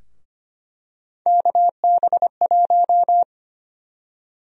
Morse code, heard as a keyed tone, about 25 wpm, 700 Hz. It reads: KB1
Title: USA Call Sign Prefixes - 25wpm